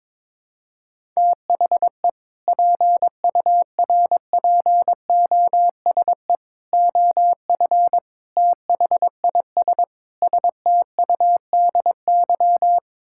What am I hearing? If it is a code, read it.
Morse code, heard as a keyed tone, about 22 wpm, 700 Hz. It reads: THE PURPOSE OF THIS STUDY
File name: practice / 22wpm / Build A Phrase - 5-Word Phrases - 22wpm